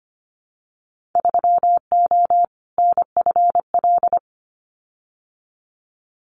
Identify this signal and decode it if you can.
Morse code, heard as a keyed tone, about 25 wpm, 700 Hz. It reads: 3O NFL
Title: Winter Field Day Exchanges 25wpm